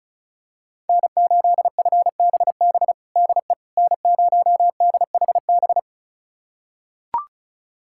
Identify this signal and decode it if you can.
Morse code, heard as a keyed tone, about 36 wpm, 700 Hz. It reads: N8FBB DE N0DHB
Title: USA Callsign DE Callsign - 5 Characters - 35wpm